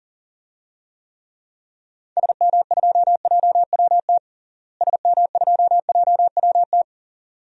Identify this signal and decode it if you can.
Morse code, heard as a keyed tone, about 40 wpm, 700 Hz. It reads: SM2JWT SM2JWT
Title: DXCC-100 Call Signs - 6 Characters - Repeated 2x - 40wpm